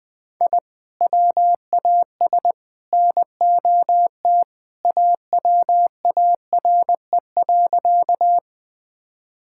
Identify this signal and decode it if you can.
Morse code, heard as a keyed tone, about 20 wpm, 700 Hz. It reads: I WAS NOT AWARE.
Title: Sentences with 3-Word Phrases - Spoken Beginning - 20wpm